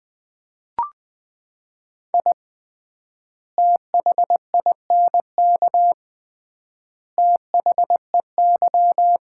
Morse code, heard as a keyed tone, about 21 wpm, 700 Hz. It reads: I THINK THEY
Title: 3x Word-Spacing - Sets of 3 Words - 20wpm